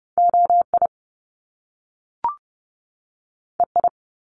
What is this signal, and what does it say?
Morse code, heard as a keyed tone, about 30 wpm, 700 Hz. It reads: OI EI
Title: Top Two Letter Combinations in Words 30wpm